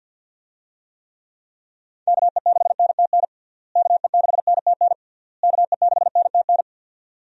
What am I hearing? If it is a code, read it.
Morse code, heard as a keyed tone, about 49 wpm, 700 Hz. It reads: KE6NTN KE6NTN KE6NTN
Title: USA Call Signs - 6 Characters - Repeated 3x - 50wpm